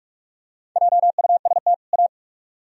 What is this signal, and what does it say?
Morse code, heard as a keyed tone, about 41 wpm, 700 Hz. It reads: JUST A